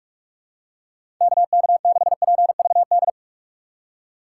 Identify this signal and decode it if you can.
Morse code, heard as a keyed tone, about 46 wpm, 700 Hz. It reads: KK6PVD
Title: USA Call Signs - 6 Characters - 3x Character Spacing - 45wpm